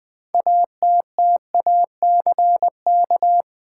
Morse code, heard as a keyed tone, about 21 wpm, 700 Hz. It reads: ATTACK
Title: Sentences from Top 1500 Words - New Word Drill - 20wpm